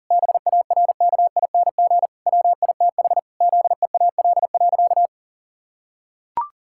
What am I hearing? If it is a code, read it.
Morse code, heard as a keyed tone, about 40 wpm, 700 Hz. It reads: BARKING WITH ZEAL.
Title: Pangram Challenge v2 - 40wpm